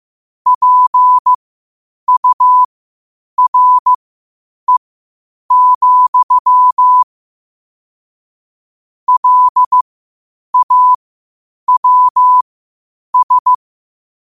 Morse code, heard as a keyed tone, about 15 wpm, spaced as for 8 wpm, 1000 Hz. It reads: PURE, LAWS
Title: 8 WPM morse code quotes for Thu, 29 Feb 2024